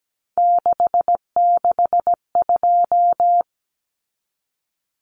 Morse code, heard as a keyed tone, about 17 wpm, 700 Hz. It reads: 662
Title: Numbers - 3-Digit 17wpm